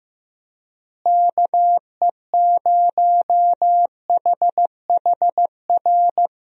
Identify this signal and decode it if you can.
Morse code, heard as a keyed tone, about 16 wpm, 700 Hz. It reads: KE0HHR
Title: USA Call Signs - 6 Characters - Repeated 3x - 15wpm